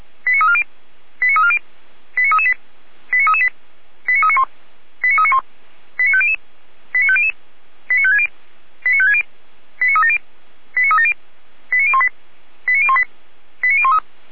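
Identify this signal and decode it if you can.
Five-tone selective call (ZVEI), heard as quick runs of five tones, tones of about 70 ms. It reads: 89340 89340 89338 89338 89391 89391 89500 89500 89560 89560 89370 89370 89018 89018 89013
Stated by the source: ZVEI five-tone selective call received on an SDR, fire brigade Orlamünde (feuerwehr-orlamuende), Germany.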